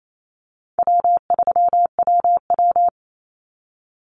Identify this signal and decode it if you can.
Morse code, heard as a keyed tone, about 28 wpm, 700 Hz. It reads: W3WW
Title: Call Signs Repeated 3x - 28wpm